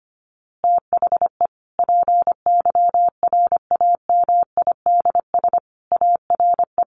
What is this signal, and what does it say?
Morse code, heard as a keyed tone, about 25 wpm, 700 Hz. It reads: THE PYRAMIDS ARE
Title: A Sound Of Thunder 25wpm